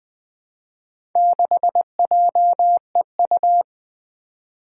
Morse code, heard as a keyed tone, about 20 wpm, 700 Hz. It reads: 6JEU
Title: Four Challenging Letter-Number Combinations 20wpm